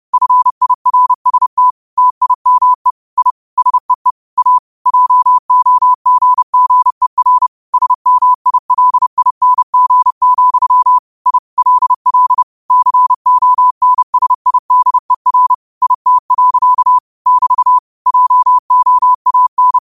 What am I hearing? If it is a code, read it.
Morse code, heard as a keyed tone, about 30 wpm, 1000 Hz. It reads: FIRST TIME I SEE A JOGGER SMILING, I LL CONSIDER IT. <BT> JOAN
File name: qotdfiles / 30wpm